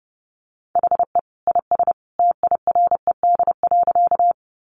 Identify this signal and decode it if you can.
Morse code, heard as a keyed tone, about 30 wpm, 700 Hz. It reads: HE IS TIRED.
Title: Additional Easy Phrases 30wpm